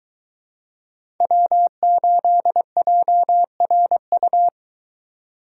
Morse code, heard as a keyed tone, about 21 wpm, 700 Hz. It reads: W8JRU